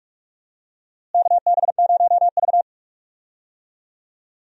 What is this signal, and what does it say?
Morse code, heard as a keyed tone, about 46 wpm, 700 Hz. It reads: KB0V
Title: USA Call Signs - 4 Characters - Repeated 3x - 45wpm